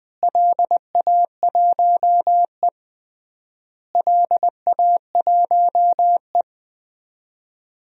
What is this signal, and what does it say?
Morse code, heard as a keyed tone, about 20 wpm, 700 Hz. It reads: LA1E LA1E
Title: DXCC-100 Call Signs - 4 Characters - Repeated 3x - 20wpm